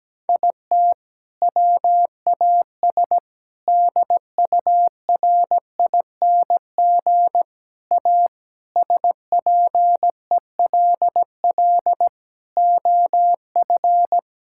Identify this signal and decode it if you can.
Morse code, heard as a keyed tone, about 17 wpm, 700 Hz. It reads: IT WAS DURING A SPELL OF
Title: Sentences from Top 1000 Words - Review Part 5 - 17wpm